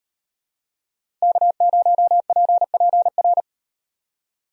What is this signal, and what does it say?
Morse code, heard as a keyed tone, about 38 wpm, 700 Hz. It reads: K0PPR